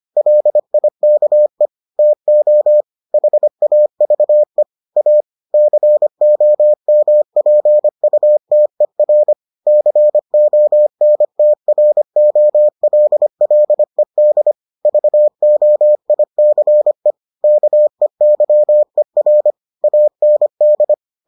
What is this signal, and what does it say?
Morse code, heard as a keyed tone, about 25 wpm, 600 Hz. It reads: LIKE TO HAVE A COMPUTER CONTROLLED VOICE KEYER AND